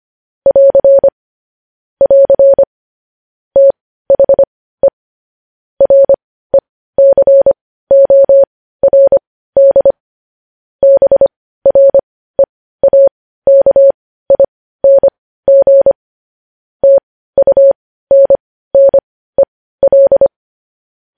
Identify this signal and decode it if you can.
Morse code, heard as a keyed tone, about 25 wpm, spaced as for 15 wpm, 550 Hz. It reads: <AR> <AR> THE RECORD BREAKING TUNNEL
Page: News Headlines in Morse Code at 15 WPM